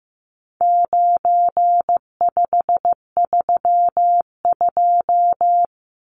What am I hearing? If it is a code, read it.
Morse code, heard as a keyed tone, about 15 wpm, 700 Hz. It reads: 9532
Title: Numbers - 4-Digit 15wpm